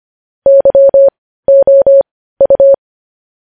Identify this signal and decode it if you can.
Morse code, heard as a keyed tone, about 25 wpm, spaced as for 15 wpm, 550 Hz. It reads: YOU
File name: podcast / News headlines-1x-25x15